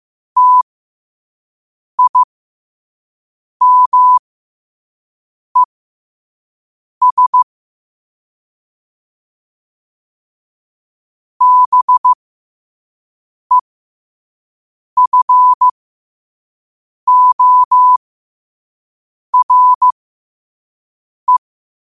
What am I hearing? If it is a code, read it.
Morse code, heard as a keyed tone, about 15 wpm, spaced as for 5 wpm, 1000 Hz. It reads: TIMES BEFORE
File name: qotdfiles / 5wpm